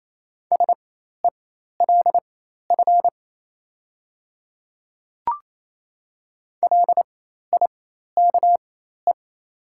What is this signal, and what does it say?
Morse code, heard as a keyed tone, about 27 wpm, spaced as for 13 wpm, 700 Hz. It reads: SELF LIKE
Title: 4x Character Spacing - Top 300 Words - 28wpm